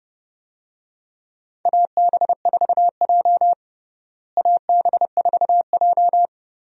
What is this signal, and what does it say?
Morse code, heard as a keyed tone, about 30 wpm, 700 Hz. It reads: AB4J AB4J
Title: USA Call Signs - 4 Characters - Repeated 3x - 30wpm